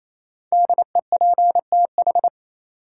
Morse code, heard as a keyed tone, about 28 wpm, 700 Hz. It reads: DEPTH
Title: Sentences from Top 1200 Words - New Word Drill - 28wpm